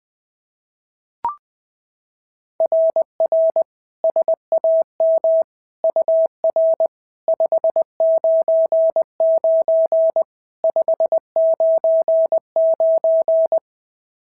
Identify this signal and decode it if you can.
Morse code, heard as a keyed tone, about 21 wpm, 690 Hz. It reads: RR SAM UR 599 599
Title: SOTA QSO Practice 20wpm